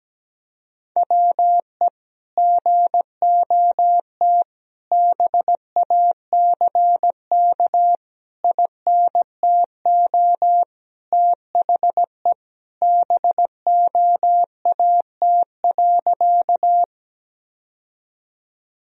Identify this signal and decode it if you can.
Morse code, heard as a keyed tone, about 17 wpm, 700 Hz. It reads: WE GOT BACK INTO THE BOAT.
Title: Sentences from Top 1000 Words - Review Part 3 - 17wpm